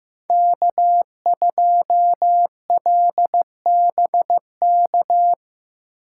Morse code, heard as a keyed tone, about 18 wpm, 700 Hz. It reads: K2LBK
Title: Speed Racing - USA 1x3 Call Signs 15wpm